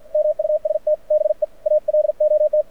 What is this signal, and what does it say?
Morse code, heard as a keyed tone, about 53 wpm, 590 Hz. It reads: MUST BE ALOT
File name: archived / CRSnet